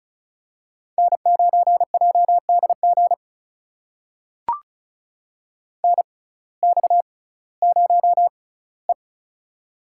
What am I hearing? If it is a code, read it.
Morse code, heard as a keyed tone, about 36 wpm, spaced as for 19 wpm, 700 Hz. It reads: N9JDG NX0E
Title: USA Call Signs - 6x Character Spacing - 35wpm